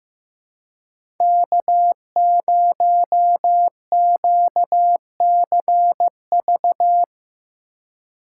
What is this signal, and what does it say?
Morse code, heard as a keyed tone, about 15 wpm, 700 Hz. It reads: K0QCV
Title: USA Call Signs - 5 Characters - 2x Character Spacing - 15wpm